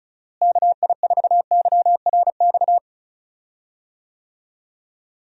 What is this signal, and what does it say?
Morse code, heard as a keyed tone, about 34 wpm, 700 Hz. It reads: KI4YRX
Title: USA Call Signs - 6 Characters - Repeated 2x - 35wpm